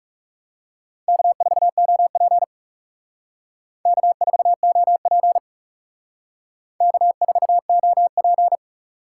Morse code, heard as a keyed tone, about 39 wpm, 700 Hz. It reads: K4OP K4OP K4OP